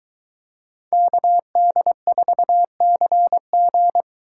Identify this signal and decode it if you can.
Morse code, heard as a keyed tone, about 22 wpm, 700 Hz. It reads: KD4CG